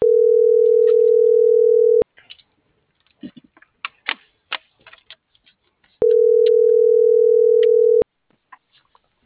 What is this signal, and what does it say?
Ringback tone (North America), 440 + 480 Hz.